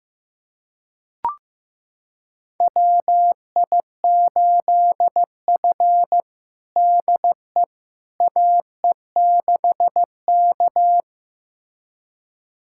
Morse code, heard as a keyed tone, about 15 wpm, 700 Hz. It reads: WI8F DE AE6K
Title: USA Callsign DE Callsign - 4 Characters - 15wpm